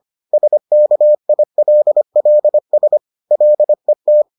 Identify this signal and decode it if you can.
Morse code, heard as a keyed tone, about 25 wpm, 600 Hz. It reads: SKILLS LET